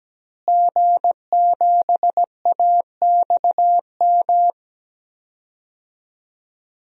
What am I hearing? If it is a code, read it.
Morse code, heard as a keyed tone, about 17 wpm, 700 Hz. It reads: G7AXM
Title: DXCC-100 Call Signs - 5 Characters - Repeated 2x - 17wpm